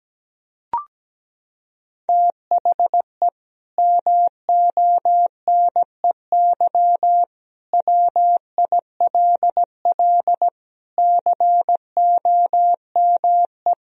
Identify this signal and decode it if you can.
Morse code, heard as a keyed tone, about 17 wpm, 700 Hz. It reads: THE MONEY WILL COME
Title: Sentences from Top 1000 Words - Review Part 4 - 17wpm